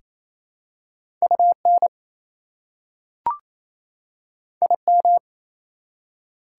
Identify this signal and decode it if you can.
Morse code, heard as a keyed tone, about 27 wpm, 700 Hz. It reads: UN IM
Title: Prefixes and Suffixes 28wpm